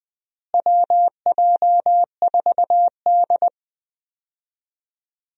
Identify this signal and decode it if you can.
Morse code, heard as a keyed tone, about 20 wpm, 700 Hz. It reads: WJ4D
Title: USA Call Signs - 4 Characters - Repeated 3x - 20wpm